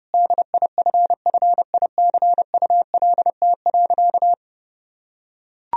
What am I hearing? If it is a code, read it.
Morse code, heard as a keyed tone, about 30 wpm, 700 Hz. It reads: DIFFICULT.
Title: Sentences from Top 1000 Words - Review Part 3 - 30wpm